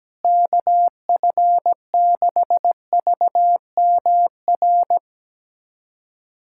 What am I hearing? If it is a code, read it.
Morse code, heard as a keyed tone, about 17 wpm, 700 Hz. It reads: KF6VMR
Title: USA Callsign DE Callsign - 6 Characters - 17wpm